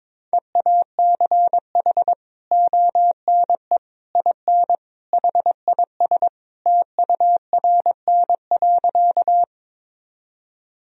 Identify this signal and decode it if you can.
Morse code, heard as a keyed tone, about 22 wpm, 700 Hz. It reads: EACH ONE IN HIS TURN.
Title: Sentences from Top 1000 Words - Review Part 1 - 22wpm